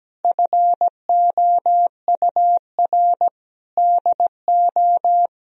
Morse code, heard as a keyed tone, about 17 wpm, 700 Hz. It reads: FOUR DO
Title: Sentences from Top 1000 Words - Review Part 1 - 17wpm